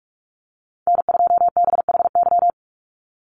Morse code, heard as a keyed tone, about 44 wpm, 700 Hz. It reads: N2BHY
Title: USA Call Signs 45wpm